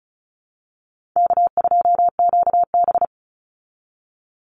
Morse code, heard as a keyed tone, about 34 wpm, 700 Hz. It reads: K2QB
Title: Easy Call Signs Repeated 2x - 35wpm